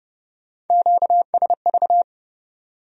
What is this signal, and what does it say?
Morse code, heard as a keyed tone, about 39 wpm, 700 Hz. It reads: QSV